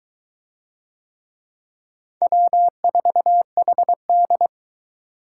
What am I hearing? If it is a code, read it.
Morse code, heard as a keyed tone, about 22 wpm, 700 Hz. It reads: W4HD